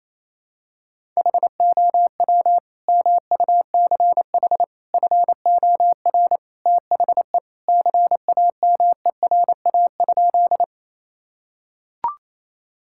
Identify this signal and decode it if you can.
Morse code, heard as a keyed tone, about 28 wpm, 700 Hz. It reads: HOW MUCH FOR THE CAMERA?
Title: Sentences from Top 1000 Words - Review Part 5 - 28wpm